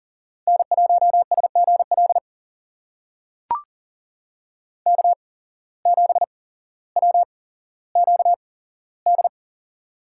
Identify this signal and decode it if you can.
Morse code, heard as a keyed tone, about 40 wpm, spaced as for 17 wpm, 700 Hz. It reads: N1SGL K7WQD
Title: USA Call Signs - 5 Characters - 8x Character Spacing - 40wpm